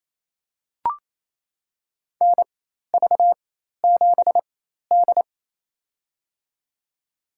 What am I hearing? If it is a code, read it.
Morse code, heard as a keyed tone, about 28 wpm, spaced as for 13 wpm, 700 Hz. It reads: NV7D
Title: USA Call Signs - 4 Characters - 4x Character Spacing - 28wpm